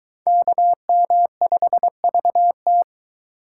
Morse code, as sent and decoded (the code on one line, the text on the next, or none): -.- -- ..... ...- -
KM5VT